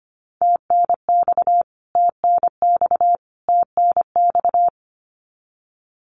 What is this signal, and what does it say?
Morse code, heard as a keyed tone, about 25 wpm, 700 Hz. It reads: TNX TNX TNX
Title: Abbreviations Repeated 3x - 25wpm